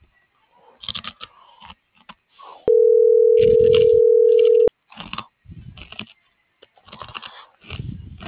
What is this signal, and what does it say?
Ringback tone (North America), 440 + 480 Hz.